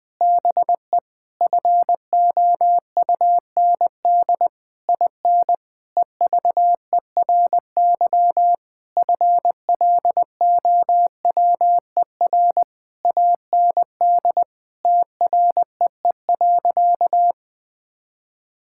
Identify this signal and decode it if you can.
Morse code, heard as a keyed tone, about 20 wpm, 700 Hz. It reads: BE FOUND IN EVERY FLOWER AND TREE.
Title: Difficult Sentences with Hints - 20wpm